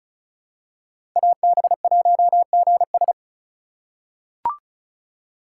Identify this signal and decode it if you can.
Morse code, heard as a keyed tone, about 36 wpm, 700 Hz. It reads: AB1GS
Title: USA Call Signs - 5 Characters - 3x Character Spacing - 35wpm